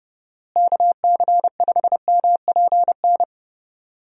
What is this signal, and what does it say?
Morse code, heard as a keyed tone, about 30 wpm, 700 Hz. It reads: KC5MPN
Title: USA Call Signs - 6 Characters - Repeated 2x - 30wpm